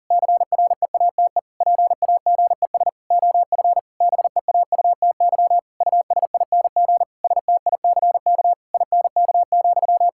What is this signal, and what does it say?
Morse code, heard as a keyed tone, about 40 wpm, 700 Hz. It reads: CREATE PAGES OF BEAUTY USING STICK INK,
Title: Pangram Challenge v2 - 40wpm